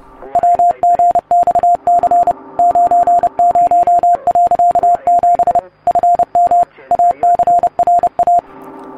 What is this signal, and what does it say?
Morse code, heard as a keyed tone, about 30 wpm, 700 Hz. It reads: WGXC 90.7 FM ACRA